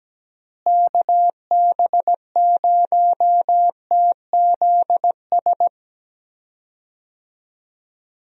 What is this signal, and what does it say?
Morse code, heard as a keyed tone, about 17 wpm, 700 Hz. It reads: KB0TZS
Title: USA Call Signs - 6 Characters - Repeated 3x - 17wpm